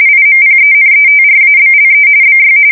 RTTY, about 45.45 baud. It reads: IMPENETRABLE BY
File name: fsk170 / USS-New-Jersey